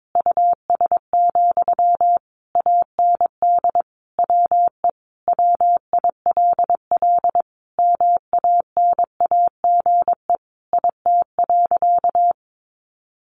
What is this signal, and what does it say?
Morse code, heard as a keyed tone, about 22 wpm, 700 Hz. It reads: US, AND WE WILL MANAGE IT.
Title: Sentences from Top 1400 Words 22wpm